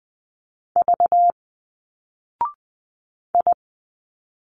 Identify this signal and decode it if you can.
Morse code, heard as a keyed tone, about 20 wpm, 700 Hz. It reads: V I
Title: Single Letters 20wpm